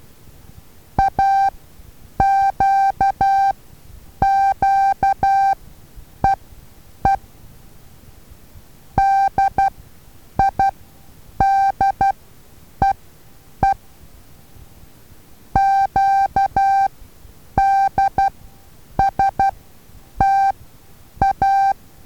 Morse code, heard as a keyed tone, about 12 wpm, spaced as for 8 wpm, 780 Hz. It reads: AQQEE DIDEE QDSTA
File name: Taviro HA6VA A